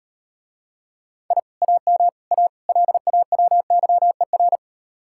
Morse code, heard as a keyed tone, about 39 wpm, 700 Hz. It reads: I AM A LAWYER